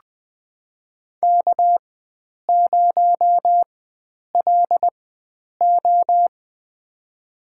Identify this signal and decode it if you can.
Morse code, heard as a keyed tone, about 20 wpm, spaced as for 9 wpm, 700 Hz. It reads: K0LO
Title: USA Call Signs - 4 Characters - 4x Character Spacing - 20wpm